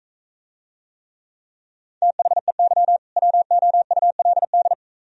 Morse code, heard as a keyed tone, about 44 wpm, 700 Hz. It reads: THEY WOULD